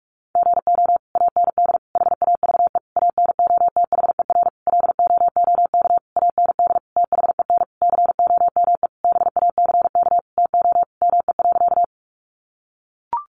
Morse code, heard as a keyed tone, about 44 wpm, 700 Hz. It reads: GO AND HAVE ANOTHER LOOK AND THEN COME BACK TO ME.
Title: Sentences from Top 600 Words 45wpm